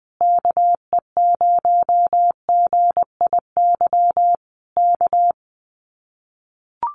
Morse code, heard as a keyed tone, about 20 wpm, 700 Hz. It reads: KE0GIY K
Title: Common QSO Phrases Encore 20wpm